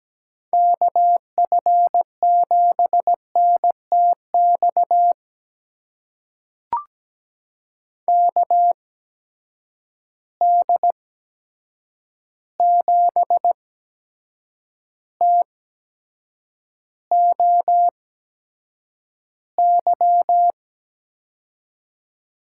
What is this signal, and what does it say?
Morse code, heard as a keyed tone, about 17 wpm, spaced as for 7 wpm, 700 Hz. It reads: KF7NTX KD7TOY
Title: USA Call Signs - 6 Characters - 8x Character Spacing - 17wpm